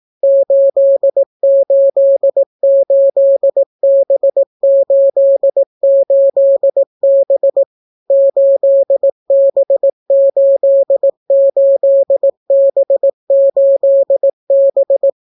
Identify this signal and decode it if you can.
Morse code, heard as a keyed tone, about 18 wpm, 550 Hz. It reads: 888B88B 8B88B8B